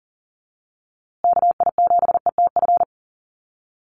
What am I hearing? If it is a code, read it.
Morse code, heard as a keyed tone, about 40 wpm, 700 Hz. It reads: KI7ETF
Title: USA Call Signs Repeated 3x 40wpm